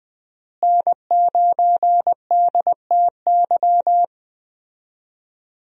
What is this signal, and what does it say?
Morse code, heard as a keyed tone, about 20 wpm, 700 Hz. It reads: N9DTY
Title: USA Call Signs - 5 Characters - Repeated 3x - 20wpm